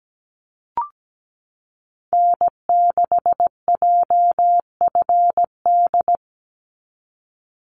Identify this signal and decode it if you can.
Morse code, heard as a keyed tone, about 17 wpm, 700 Hz. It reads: N6JFD
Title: NA SOTA Activator Callsigns 17wpm